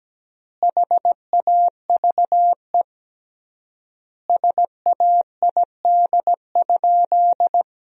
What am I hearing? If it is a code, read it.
Morse code, heard as a keyed tone, about 17 wpm, 700 Hz. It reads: HAVE SAID?
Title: 3x Word-Spacing - Sentences from Top 100 Words - 17wpm